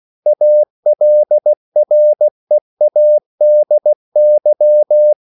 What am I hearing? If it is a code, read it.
Morse code, heard as a keyed tone, about 16 wpm, 600 Hz. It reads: ALREADY
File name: CW / Practice / EnglishQuotations0216